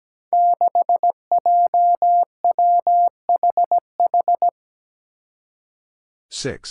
Morse code, heard as a keyed tone, about 17 wpm, 700 Hz. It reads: 6JWHH
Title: Five Challenging Letter-Number Combinations 17wpm